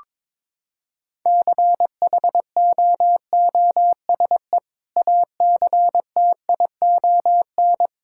Morse code, heard as a keyed tone, about 22 wpm, 700 Hz. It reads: CHOOSE ACTION
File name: practice / 22wpm / Sentences with 2-Word Phrases - 22wpm